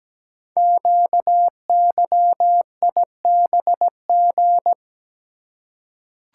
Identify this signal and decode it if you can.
Morse code, heard as a keyed tone, about 17 wpm, 700 Hz. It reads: QYIBG
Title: Code Groups - 5 Letters - 17wpm